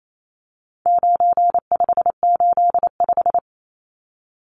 Morse code, heard as a keyed tone, about 28 wpm, 700 Hz. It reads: 9585
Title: Numbers - 4-Digit 28wpm